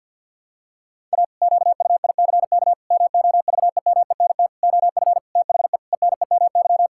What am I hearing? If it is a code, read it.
Morse code, heard as a keyed tone, about 53 wpm, 700 Hz. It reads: A QUICK MOVEMENT OF THE ENEMY
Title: Pangram Challenge v2 - 50wpm